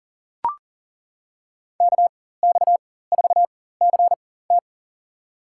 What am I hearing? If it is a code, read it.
Morse code, heard as a keyed tone, about 40 wpm, spaced as for 19 wpm, 700 Hz. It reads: KX4CT
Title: USA Call Signs - 5 Characters - 4x Character Spacing - 40wpm